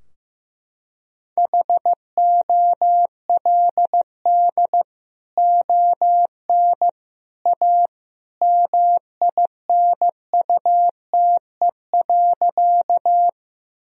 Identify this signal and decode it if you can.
Morse code, heard as a keyed tone, about 16 wpm, 700 Hz. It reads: HOLD ON A MINUTE.
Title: Sentences from Top 1000 Words - Review Part 2 - 15wpm